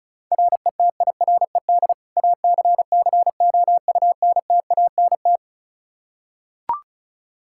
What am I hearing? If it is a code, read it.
Morse code, heard as a keyed tone, about 34 wpm, 700 Hz. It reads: RETIRED ACCOUNTANT
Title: Practice QSO - Exchange 3 - More Info - 35wpm